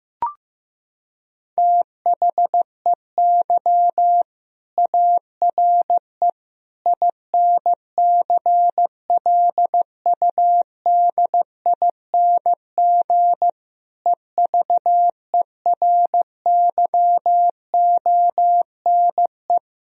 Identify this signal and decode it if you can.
Morse code, heard as a keyed tone, about 15 wpm, 700 Hz. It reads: THEY ARE INCLUDING EVERYONE
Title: Sentences from Top 1000 Words - Review Part 4 - 15wpm